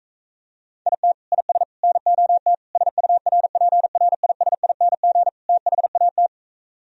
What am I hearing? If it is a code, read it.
Morse code, heard as a keyed tone, about 45 wpm, 700 Hz. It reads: IT IS NOT SURPRISING THAT